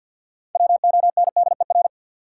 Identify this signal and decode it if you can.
Morse code, heard as a keyed tone, about 51 wpm, 700 Hz. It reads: WONDER
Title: Sentences from Top 500 Words - New Word Drill - 50wpm